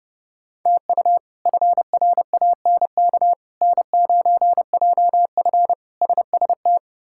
Morse code, heard as a keyed tone, about 30 wpm, 700 Hz. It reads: TU FRANK N9JF SST
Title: K1USN SST QSO Practice - 30wpm